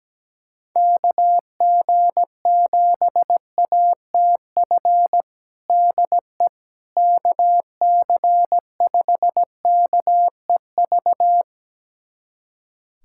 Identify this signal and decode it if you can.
Morse code, heard as a keyed tone, about 17 wpm, 700 Hz. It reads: KG7ATF DE KC5KEV
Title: USA Callsign DE Callsign - 6 Characters - 17wpm